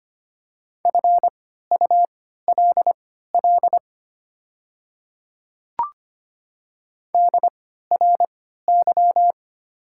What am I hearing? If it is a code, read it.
Morse code, heard as a keyed tone, about 25 wpm, spaced as for 14 wpm, 700 Hz. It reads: FULL DRY
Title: 3x Character Spacing - Top 500 Words - 25wpm